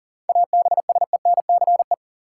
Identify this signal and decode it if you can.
Morse code, heard as a keyed tone, about 39 wpm, 700 Hz. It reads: ABSENCE
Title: Sets of 4 Words - New Word Drill - 40wpm